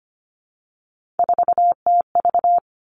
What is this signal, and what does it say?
Morse code, heard as a keyed tone, about 25 wpm, 700 Hz. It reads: 4TV
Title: Three Challenging Letter-Number Combinations 25wpm